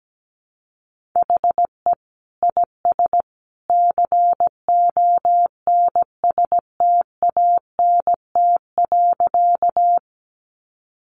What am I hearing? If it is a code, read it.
Morse code, heard as a keyed tone, about 17 wpm, 700 Hz. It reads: HE IS CONSTANT.
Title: Sentences from Top 1500 Words 17wpm